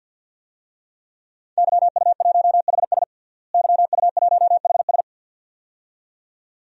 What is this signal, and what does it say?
Morse code, heard as a keyed tone, about 52 wpm, 700 Hz. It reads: YU1HS YU1HS
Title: DXCC-100 Call Signs - 5 Characters - Repeated 2x - 50wpm